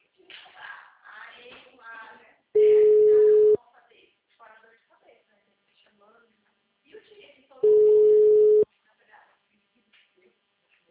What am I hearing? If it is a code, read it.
Ringback tone (Europe, 425 Hz), with speech.